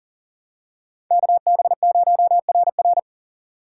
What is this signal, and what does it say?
Morse code, heard as a keyed tone, about 37 wpm, 700 Hz. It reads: KB0RR
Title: USA Call Signs - 5 Characters - 6x Character Spacing - 40wpm